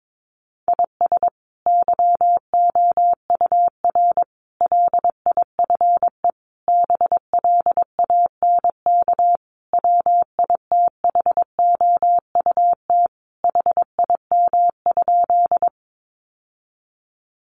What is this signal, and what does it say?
Morse code, heard as a keyed tone, about 22 wpm, 700 Hz. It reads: IS YOUR LIFE BLANK WITHOUT HIM?
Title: Sentences from Top 1500 Words 22wpm